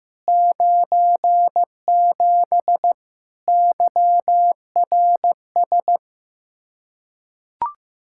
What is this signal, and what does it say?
Morse code, heard as a keyed tone, about 15 wpm, 700 Hz. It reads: 97 YRS
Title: Practice QSO - Exchange 3 - More Info - 15wpm